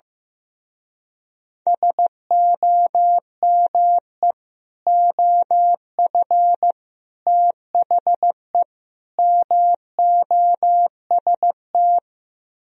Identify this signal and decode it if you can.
Morse code, heard as a keyed tone, about 15 wpm, 700 Hz. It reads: SOME OF THE MOST
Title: Speed-Racing - Sets of 4 Words 15wpm